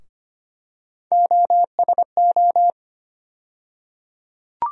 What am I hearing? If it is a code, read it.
Morse code, heard as a keyed tone, about 25 wpm, 700 Hz. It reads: OSO
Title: USA Call Sign Postfixes - 3 Letters 25wpm